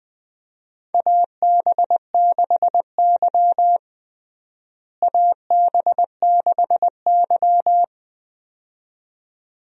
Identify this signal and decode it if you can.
Morse code, heard as a keyed tone, about 21 wpm, 700 Hz. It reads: AB6Y AB6Y
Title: USA Call Signs - 4 Characters - Repeated 2x - 20wpm